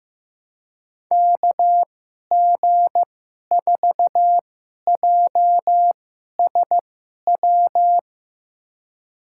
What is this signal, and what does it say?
Morse code, heard as a keyed tone, about 15 wpm, spaced as for 11 wpm, 700 Hz. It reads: KG4JSW
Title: USA Call Signs - 6 Characters - 2x Character Spacing - 15wpm